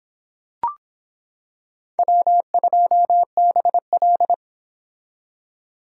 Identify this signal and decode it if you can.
Morse code, heard as a keyed tone, about 25 wpm, 700 Hz. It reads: W2BL